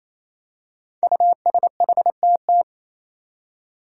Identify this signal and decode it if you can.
Morse code, heard as a keyed tone, about 28 wpm, 700 Hz. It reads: USHTT
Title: Five Challenging Letter-Number Combinations 28wpm